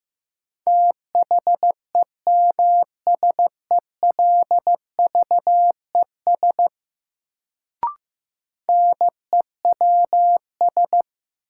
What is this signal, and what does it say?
Morse code, heard as a keyed tone, about 15 wpm, 700 Hz. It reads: THEMSELVES NEWS
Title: Sentences from Top 800 Words - New Word Drill - 15wpm